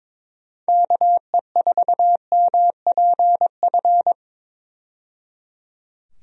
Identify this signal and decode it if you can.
Morse code, heard as a keyed tone, about 22 wpm, 700 Hz. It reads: KE4MPF
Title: USA Call Signs - 6 Characters - 22wpm